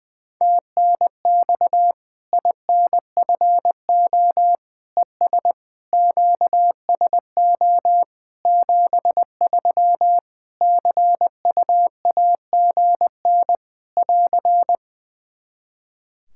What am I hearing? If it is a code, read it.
Morse code, heard as a keyed tone, about 20 wpm, 700 Hz. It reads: TNX INFO ES QSO 73 CUAGN <AR>
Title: Practice QSO - Exchange 4 - Ending - 20wpm